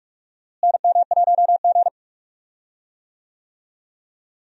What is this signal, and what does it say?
Morse code, heard as a keyed tone, about 46 wpm, 700 Hz. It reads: NM1G
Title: USA Call Signs - 4 Characters - Repeated 2x - 45wpm